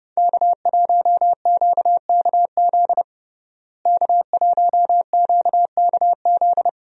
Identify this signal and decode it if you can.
Morse code, heard as a keyed tone, about 32 wpm, 700 Hz. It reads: K1QKZ K1QKZ
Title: USA Call Signs - 5 Characters - Repeated 2x - 30wpm